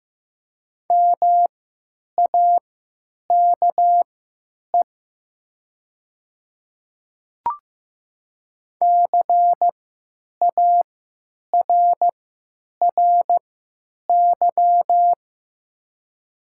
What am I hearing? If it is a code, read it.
Morse code, heard as a keyed tone, about 15 wpm, spaced as for 9 wpm, 700 Hz. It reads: MAKE CARRY
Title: 3x Character Spacing - Top 500 Words - 15wpm